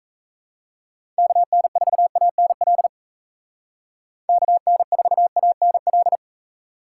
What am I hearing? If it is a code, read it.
Morse code, heard as a keyed tone, about 35 wpm, 700 Hz. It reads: KN4ANL KN4ANL
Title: Speed Racing - USA 2x3 Call Signs 28wpm